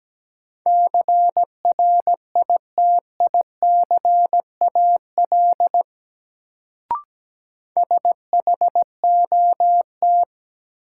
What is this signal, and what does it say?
Morse code, heard as a keyed tone, about 17 wpm, 700 Hz. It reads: CRITICAL SHOT
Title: Sentences from Top 1100 Words - New Word Drill Alt - 17wpm